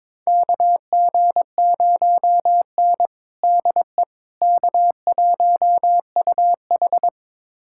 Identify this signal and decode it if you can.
Morse code, heard as a keyed tone, about 22 wpm, 700 Hz. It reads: KG0N DE K1UH
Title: USA Callsign DE Callsign - 4 Characters - 22wpm